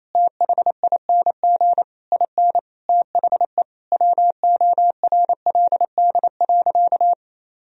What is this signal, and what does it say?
Morse code, heard as a keyed tone, about 27 wpm, 700 Hz. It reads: THING IN THE WORLD.
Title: Sentences from Top 1000 Words - Review Part 4 - 28wpm